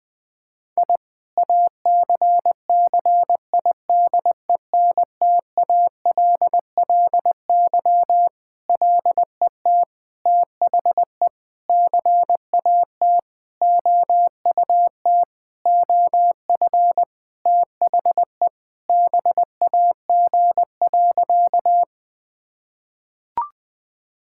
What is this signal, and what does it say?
Morse code, heard as a keyed tone, about 20 wpm, 700 Hz. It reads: I ACCIDENTALLY LET THE CAT OUT OF THE BAG.
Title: Sentences with 7-Word Phrases - 20wpm